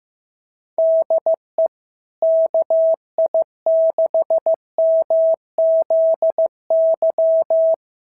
Morse code, heard as a keyed tone, about 14 wpm, 650 Hz. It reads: DE KI6MZY
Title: SOTA QSO Practice 15wpm